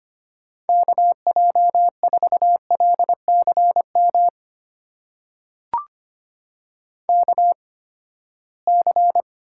Morse code, heard as a keyed tone, about 25 wpm, spaced as for 17 wpm, 700 Hz. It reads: KJ4LCM KC
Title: USA Call Signs - 6 Characters - 8x Character Spacing - 25wpm